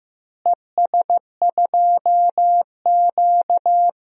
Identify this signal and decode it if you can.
Morse code, heard as a keyed tone, about 15 wpm, 700 Hz. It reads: ES2Q
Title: DXCC-100 Call Signs - 4 Characters - Repeated 2x - 15wpm